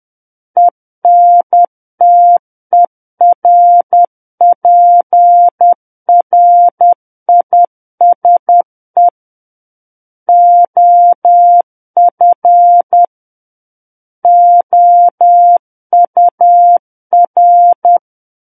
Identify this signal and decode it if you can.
Morse code, heard as a keyed tone, about 10 wpm, 700 Hz. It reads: ENTERPRISE OF OUR
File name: Book One / 01-Chapter 1 - 10 WPM